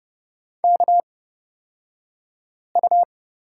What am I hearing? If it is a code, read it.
Morse code, heard as a keyed tone, about 29 wpm, spaced as for 5 wpm, 700 Hz. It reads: KU
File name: Lesson 37d - Letter x - Two Characters 30wpm